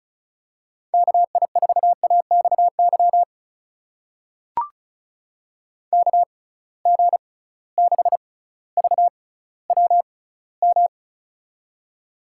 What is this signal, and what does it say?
Morse code, heard as a keyed tone, about 34 wpm, spaced as for 18 wpm, 700 Hz. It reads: KI4AXY KG6VWM
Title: USA Call Signs - 6 Characters - 6x Character Spacing - 35wpm